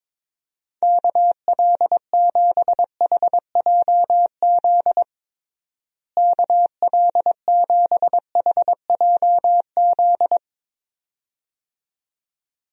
Morse code, heard as a keyed tone, about 22 wpm, 700 Hz. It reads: KL7HJZ KL7HJZ
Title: DXCC-100 Call Signs - 6 Characters - Repeated 2x - 22wpm